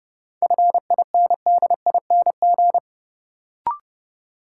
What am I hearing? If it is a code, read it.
Morse code, heard as a keyed tone, about 30 wpm, 700 Hz. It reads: FINDING
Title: Sentences from Top 1100 Words - New Word Drill Alt - 30wpm